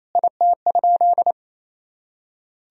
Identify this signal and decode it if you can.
Morse code, heard as a keyed tone, about 28 wpm, 700 Hz. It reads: IT?
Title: Sentences from Top 1000 Words - Review Part 2 - 28wpm